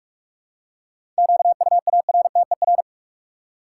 Morse code, heard as a keyed tone, about 42 wpm, 700 Hz. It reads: QUARTER
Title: Seven Letter Words - 45wpm